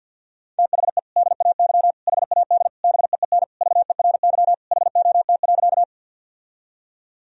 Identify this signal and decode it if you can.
Morse code, heard as a keyed tone, about 52 wpm, 700 Hz. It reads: THE DAY HAD BEEN VERY HOT.
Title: Sentences from Top 1000 Words - Review Part 3 - 50wpm